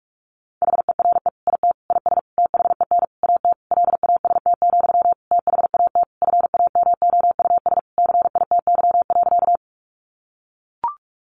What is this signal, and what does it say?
Morse code, heard as a keyed tone, about 44 wpm, 700 Hz. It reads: HERE IT IS THEN AT LAST, THAT FAMOUS CITY.
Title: Sentences from Top 1500 Words 45wpm